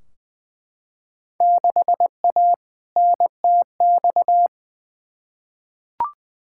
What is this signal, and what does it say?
Morse code, heard as a keyed tone, about 20 wpm, 700 Hz. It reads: 6A NTX
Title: Field Day Exchanges - 20wpm